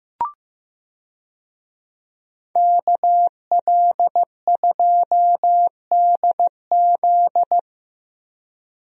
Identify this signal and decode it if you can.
Morse code, heard as a keyed tone, about 15 wpm, 700 Hz. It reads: KL2DZ
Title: USA Call Signs - 5 Characters - Repeated 2x - 15wpm